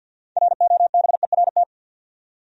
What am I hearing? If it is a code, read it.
Morse code, heard as a keyed tone, about 53 wpm, 700 Hz. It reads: ROBERT
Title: USA Common Names - 50wpm